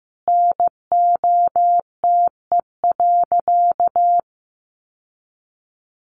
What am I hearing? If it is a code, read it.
Morse code, heard as a keyed tone, about 15 wpm, 700 Hz. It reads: NOTE.
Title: Sentences from Top 1500 Words 15wpm